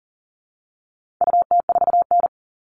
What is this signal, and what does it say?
Morse code, heard as a keyed tone, about 40 wpm, 700 Hz. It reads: UT4N
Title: All Cut Numbers - 4-Digit 40wpm